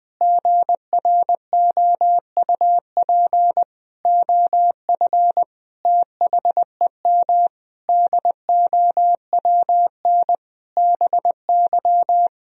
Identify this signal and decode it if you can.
Morse code, heard as a keyed tone, about 20 wpm, 700 Hz. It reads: GROUP OF THEM DOWN BY
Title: Sentences from Top 1000 Words - Review Part 1 - 20wpm